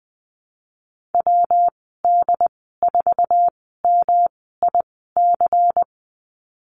Morse code, heard as a keyed tone, about 20 wpm, spaced as for 15 wpm, 700 Hz. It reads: WD4MIC
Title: 173 Lesson 28g - Number 9 - Call Signs 20wpm